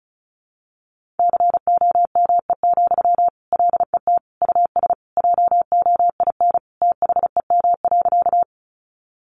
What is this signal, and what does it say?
Morse code, heard as a keyed tone, about 34 wpm, 700 Hz. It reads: COME, LET US JOIN THEM.
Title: Sentences from Top 1400 Words 35wpm